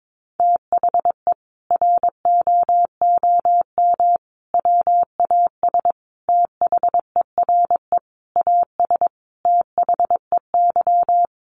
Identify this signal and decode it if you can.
Morse code, heard as a keyed tone, about 22 wpm, 700 Hz. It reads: THE ROOM WAS THERE AS THEY
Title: A Sound Of Thunder 22wpm